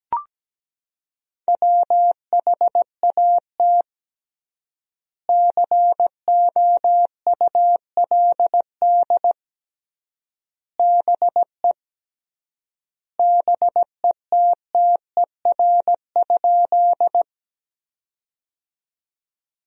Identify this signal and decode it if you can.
Morse code, heard as a keyed tone, about 17 wpm, 700 Hz. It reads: WHAT COULD BE BETTER?
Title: 3x Word-Spacing - Sentences from Top 400 Words - 17wpm